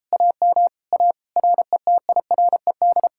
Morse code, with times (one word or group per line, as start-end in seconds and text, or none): 0.1-0.7 AM
0.9-1.1 A
1.4-3.1 RETIRED